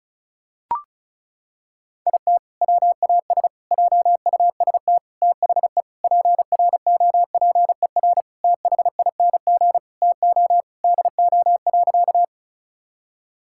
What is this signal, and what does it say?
Morse code, heard as a keyed tone, about 36 wpm, 700 Hz. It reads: IT WAS JUST THE PROPER THING TO DO.
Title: Sentences from Top 1000 Words - Review Part 5 - 35wpm